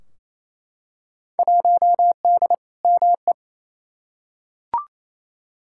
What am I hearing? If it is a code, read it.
Morse code, heard as a keyed tone, about 28 wpm, 700 Hz. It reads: 1D ME
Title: Field Day Exchanges Encore - 28wpm